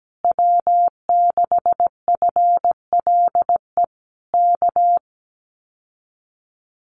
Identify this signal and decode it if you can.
Morse code, heard as a keyed tone, about 17 wpm, 700 Hz. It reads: W6FLE K
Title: Common QSO Phrases Encore 17wpm